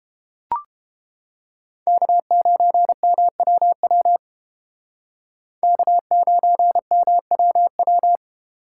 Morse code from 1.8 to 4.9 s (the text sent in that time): K9MWW